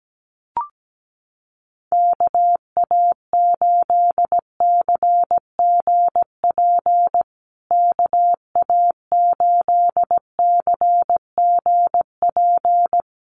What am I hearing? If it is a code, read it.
Morse code, heard as a keyed tone, about 17 wpm, 700 Hz. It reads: KA8CGP KA8CGP
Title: USA Call Signs Repeated 2x 17wpm